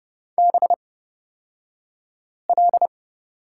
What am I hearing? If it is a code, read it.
Morse code, heard as a keyed tone, about 30 wpm, spaced as for 5 wpm, 700 Hz. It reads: BL
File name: Lesson 36d - BT - Two Characters 30wpm